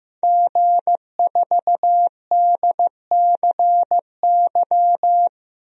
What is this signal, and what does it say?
Morse code, heard as a keyed tone, about 15 wpm, 700 Hz. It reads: G4DCY
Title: DXCC-100 Call Signs - 5 Characters - 15wpm